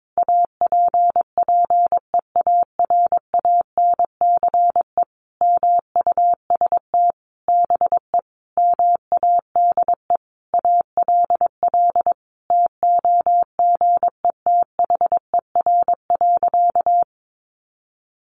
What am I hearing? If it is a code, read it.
Morse code, heard as a keyed tone, about 22 wpm, 700 Hz. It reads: APPEARANCE MUST BE MADE ALL TOGETHER.
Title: Sentences from Top 1100 Words 22wpm